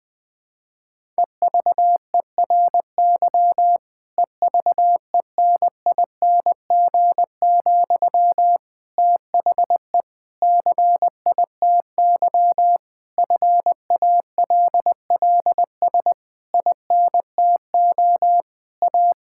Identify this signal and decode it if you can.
Morse code, heard as a keyed tone, about 20 wpm, 700 Hz. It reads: EVERY EVENING, THE CITY FALLS INTO A
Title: Difficult Sentences with Hints - 20wpm